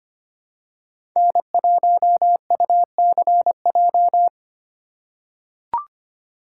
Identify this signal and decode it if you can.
Morse code, heard as a keyed tone, about 25 wpm, 700 Hz. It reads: N1UCJ
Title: USA Call Signs - 5 Characters - 2x Character Spacing - 25wpm